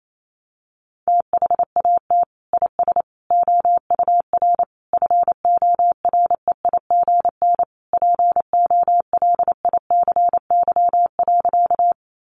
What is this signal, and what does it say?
Morse code, heard as a keyed tone, about 28 wpm, 700 Hz. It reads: THAT IS OUR FOREIGN POLICY.
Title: Sentences from Top 1400 Words 28wpm